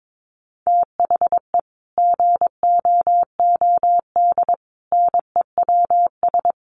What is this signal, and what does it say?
Morse code, heard as a keyed tone, about 22 wpm, 700 Hz. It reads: THE GOOD NEWS
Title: Sets of 5 Words 22wpm